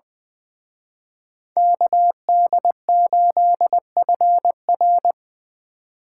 Morse code, heard as a keyed tone, about 24 wpm, 700 Hz. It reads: KD8FR
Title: Speed Racing - USA 2x2 Call Signs 20wpm